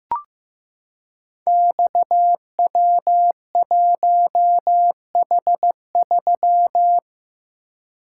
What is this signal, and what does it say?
Morse code, heard as a keyed tone, about 15 wpm, 700 Hz. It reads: XW1H3
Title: Code Groups - 5 Letter:Numbers - 15wpm